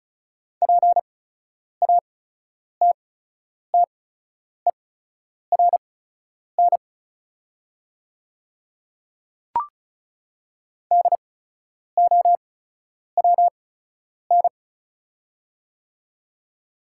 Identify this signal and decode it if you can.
Morse code, heard as a keyed tone, about 36 wpm, spaced as for 10 wpm, 700 Hz. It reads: PATTERN DOWN
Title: 8x Character Spacing - Top 500 Words - 35wpm